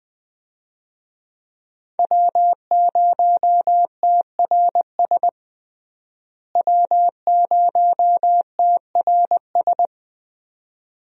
Morse code, heard as a keyed tone, about 21 wpm, 700 Hz. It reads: W0TRS W0TRS
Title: USA Call Signs - 5 Characters - Repeated 2x - 20wpm